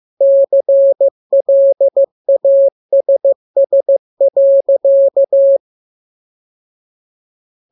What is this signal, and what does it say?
Morse code, heard as a keyed tone, about 15 wpm, 550 Hz. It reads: CLASS.
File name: jokes / joke 3 15 0000